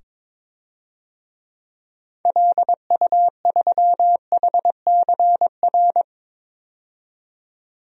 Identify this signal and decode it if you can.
Morse code, heard as a keyed tone, about 22 wpm, 700 Hz. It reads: LU3HCR
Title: DXCC-100 Call Signs - 6 Characters - 22wpm